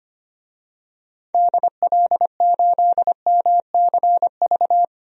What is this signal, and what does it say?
Morse code, heard as a keyed tone, about 25 wpm, 700 Hz. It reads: DL8MCV
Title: DXCC-100 Call Signs - 6 Characters - 25wpm